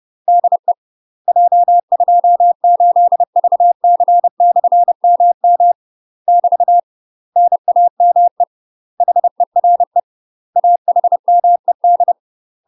Morse code, heard as a keyed tone, about 30 wpm, 700 Hz. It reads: DE J28VC/MM <BT> NAME HERE AHMED
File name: UK MASTER QSOs TRAINING / 30WPM / UK MASTER QSOs 0022